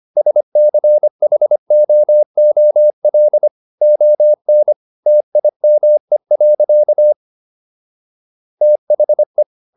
Morse code, heard as a keyed tone, about 25 wpm, 600 Hz. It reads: SCHOOL ON TIME. THE